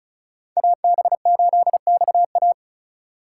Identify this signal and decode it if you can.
Morse code, heard as a keyed tone, about 36 wpm, 690 Hz. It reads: AB8XA
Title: USA Call Signs - 5 Characters - 3x Character Spacing - 35wpm